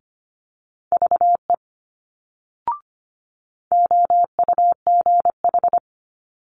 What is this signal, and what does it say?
Morse code, heard as a keyed tone, about 25 wpm, 700 Hz. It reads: VE OUGH
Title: Multi-letter Phonemes Quick Fire 25wpm